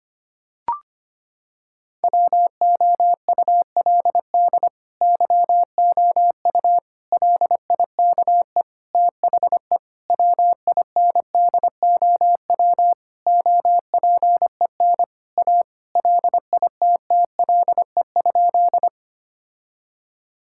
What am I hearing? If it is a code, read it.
Morse code, heard as a keyed tone, about 25 wpm, 700 Hz. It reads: WOULD YOU LIKE THE WINDOW OPEN A LITTLE?
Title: Sentences from Top 1600 Words 25wpm